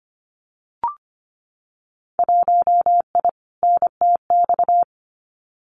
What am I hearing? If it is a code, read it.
Morse code, heard as a keyed tone, about 25 wpm, 700 Hz. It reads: 1I NTX
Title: Winter Field Day Exchanges Encore 25wpm